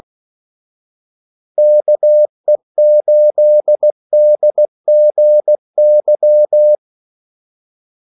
Morse code, heard as a keyed tone, about 16 wpm, 600 Hz. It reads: KE8DGY
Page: Kaldesignaler 16 wpm